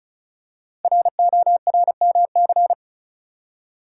Morse code, heard as a keyed tone, about 36 wpm, 700 Hz. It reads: RORMC
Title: Code Groups - 5 Letters - 35wpm